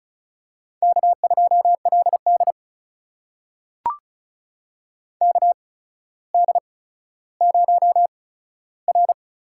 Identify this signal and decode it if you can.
Morse code, heard as a keyed tone, about 36 wpm, spaced as for 15 wpm, 700 Hz. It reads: K2LD KD0R
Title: USA Call Signs - 4 Characters - 8x Character Spacing - 35wpm